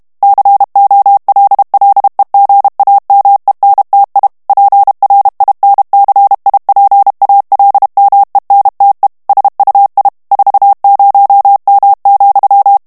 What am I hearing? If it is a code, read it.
Morse code, heard as a keyed tone, about 32 wpm, 800 Hz. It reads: COLLEGAMENTI PRINCIPALMENTE SUI 40M,
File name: ITALIEN texte clair 1 vitesse 30 mots minute